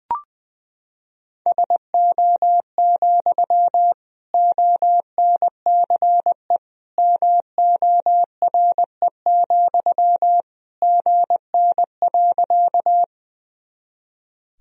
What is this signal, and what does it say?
Morse code, heard as a keyed tone, about 21 wpm, 700 Hz. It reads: SO, ONCE MORE, GN.
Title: Sentences from Top 300 Words and Abbreviations 20wpm